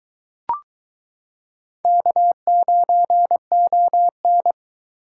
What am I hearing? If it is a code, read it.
Morse code, heard as a keyed tone, about 22 wpm, 700 Hz. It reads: K9ON